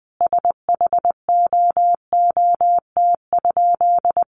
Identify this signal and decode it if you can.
Morse code, heard as a keyed tone, about 20 wpm, 700 Hz. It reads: SHOOT?
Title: A Sound Of Thunder 20wpm